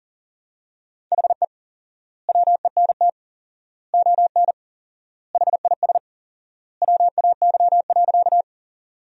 Morse code, heard as a keyed tone, about 40 wpm, 700 Hz. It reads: HE WENT ON HIS WAY.
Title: 4x Word-Spacing - Sentences from Top 200 Words - 40wpm